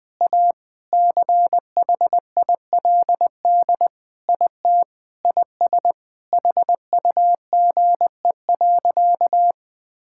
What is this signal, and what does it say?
Morse code, heard as a keyed tone, about 20 wpm, 700 Hz. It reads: A CHILD IT IS HUGE.
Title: Sentences with 5-Word Phrases - 20wpm